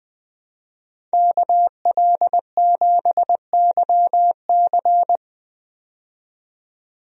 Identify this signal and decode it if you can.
Morse code, heard as a keyed tone, about 20 wpm, 700 Hz. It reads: KL7YC
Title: DXCC-100 Call Signs - 5 Characters - 20wpm